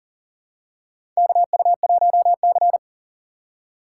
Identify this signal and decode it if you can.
Morse code, heard as a keyed tone, about 40 wpm, 700 Hz. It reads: KU1C
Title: USA Call Signs - 4 Characters - 3x Character Spacing - 40wpm